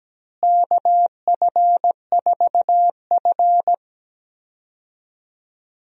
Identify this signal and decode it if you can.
Morse code, heard as a keyed tone, about 17 wpm, 700 Hz. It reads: KF4F
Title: USA Call Signs - 4 Characters - Repeated 3x - 17wpm